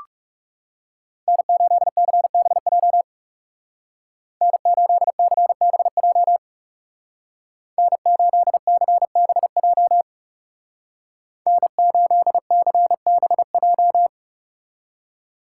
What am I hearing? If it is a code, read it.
Morse code, heard as a keyed tone, about 34 wpm, 700 Hz. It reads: N8CBJ N8CBJ N8CBJ N8CBJ
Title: Speed Racing - USA 1x3 Call Signs 30wpm